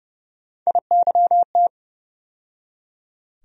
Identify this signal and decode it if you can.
Morse code, heard as a keyed tone, about 30 wpm, 700 Hz. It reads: IYT
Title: USA Call Sign Postfixes - 3 Letters 30wpm